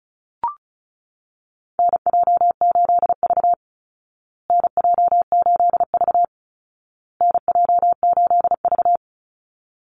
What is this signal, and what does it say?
Morse code, heard as a keyed tone, about 34 wpm, 700 Hz. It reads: NJ8V NJ8V NJ8V
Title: Call Signs Repeated 2x - 35wpm